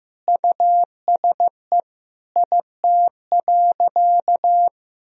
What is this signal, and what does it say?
Morse code, heard as a keyed tone, about 14 wpm, 700 Hz. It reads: USE IT.
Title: Sentences from Top 500 Words and Abbreviations 15wpm